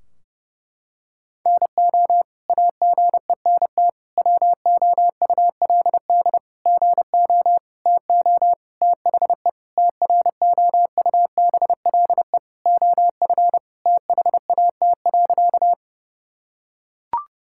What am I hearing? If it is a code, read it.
Morse code, heard as a keyed tone, about 30 wpm, 700 Hz. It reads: NO AGENT WOULD GO TO THE TROUBLE OF THAT.
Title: Sentences from Top 1600 Words 30wpm